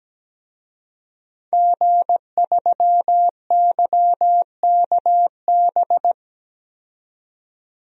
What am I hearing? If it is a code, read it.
Morse code, heard as a keyed tone, about 17 wpm, 700 Hz. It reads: G3YKB
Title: DXCC-100 Call Signs - 5 Characters - 17wpm